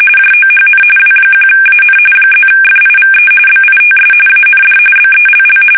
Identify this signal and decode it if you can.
RTTY, about 45.45 baud. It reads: JOHN W1AN +FIVE BITS AND BELL
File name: fsk850 / teletype-RTTY